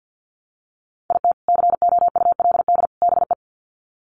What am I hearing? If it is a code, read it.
Morse code, heard as a keyed tone, about 50 wpm, 700 Hz. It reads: IT COULD BE
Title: Sets of 3 Words 50wpm